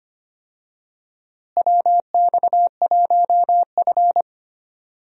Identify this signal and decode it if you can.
Morse code, heard as a keyed tone, about 25 wpm, 700 Hz. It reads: WX1F
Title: USA Call Signs - 4 Characters - Repeated 3x - 25wpm